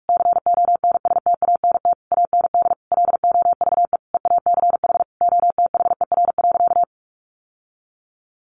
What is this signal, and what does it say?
Morse code, heard as a keyed tone, about 44 wpm, 700 Hz. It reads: CONSTANT AND LOVE EACH OTHER.
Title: Sentences from Top 1500 Words 45wpm